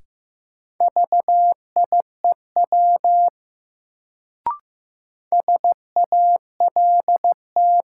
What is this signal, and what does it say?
Morse code, heard as a keyed tone, about 15 wpm, 700 Hz. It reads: VIEW SALT
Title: Sets of 5 Words - New Word Drill - 15wpm